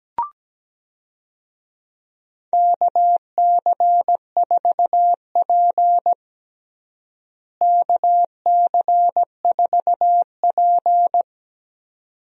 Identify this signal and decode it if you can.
Morse code, heard as a keyed tone, about 17 wpm, 700 Hz. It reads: KC4P KC4P
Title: USA Call Signs - 4 Characters - Repeated 2x - 17wpm